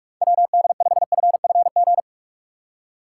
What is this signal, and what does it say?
Morse code, heard as a keyed tone, about 42 wpm, 700 Hz. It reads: WD5FFG
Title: USA Call Signs - 6 Characters - 4x Character Spacing - 45wpm